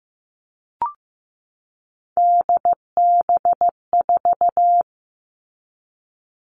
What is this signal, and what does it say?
Morse code, heard as a keyed tone, about 15 wpm, 700 Hz. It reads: DB4
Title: All Cut Numbers - 3-Digit 15wpm